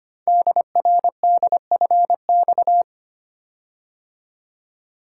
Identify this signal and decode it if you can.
Morse code, heard as a keyed tone, about 25 wpm, 700 Hz. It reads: DRDFX
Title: Code Groups - 5 Letters - 25wpm